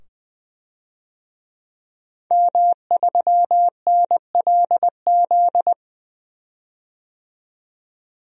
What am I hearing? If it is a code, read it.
Morse code, heard as a keyed tone, about 21 wpm, 700 Hz. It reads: M3NLZ
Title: DXCC-100 Call Signs - 5 Characters - 20wpm